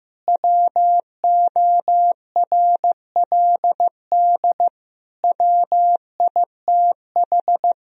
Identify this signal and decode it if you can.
Morse code, heard as a keyed tone, about 16 wpm, 700 Hz. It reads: WORLD WITH
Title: Sentences from Top 1000 Words - Review Part 3 - 15wpm